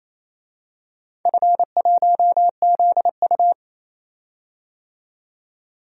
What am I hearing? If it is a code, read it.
Morse code, heard as a keyed tone, about 28 wpm, 700 Hz. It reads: F1ZU
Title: DXCC-100 Call Signs - 4 Characters - 28wpm